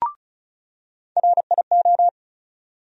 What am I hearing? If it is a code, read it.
Morse code, heard as a keyed tone, about 36 wpm, 700 Hz. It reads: RIO
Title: USA Call Sign Postfixes - 3 Letters 35wpm